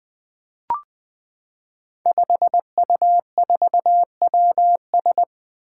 Morse code, heard as a keyed tone, about 20 wpm, 700 Hz. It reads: 5U4WS
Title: Five Challenging Letter-Number Combinations 20wpm